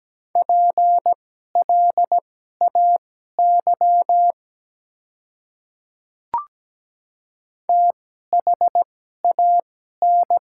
Morse code, heard as a keyed tone, about 17 wpm, spaced as for 12 wpm, 700 Hz. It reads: PLAY THAN
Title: 2x Character Spacing - Top 200 Words - 17wpm